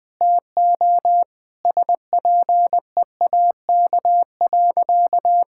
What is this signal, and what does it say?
Morse code, heard as a keyed tone, about 20 wpm, 700 Hz. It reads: TO SPEAK.
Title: Sentences with 4-Word Phrases - 20wpm